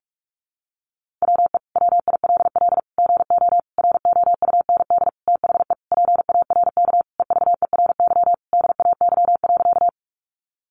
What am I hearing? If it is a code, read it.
Morse code, heard as a keyed tone, about 44 wpm, 700 Hz. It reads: WE WILL GO ROUND THE PARK EVERY DAY.
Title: Sentences from Top 1300 Words 45wpm